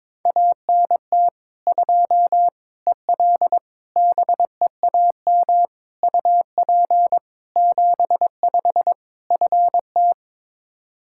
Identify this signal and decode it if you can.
Morse code, heard as a keyed tone, about 21 wpm, 700 Hz. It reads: ANT 2 EL BEAM UP 75 FT
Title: Practice QSO - Exchange 2 - Info - 22wpm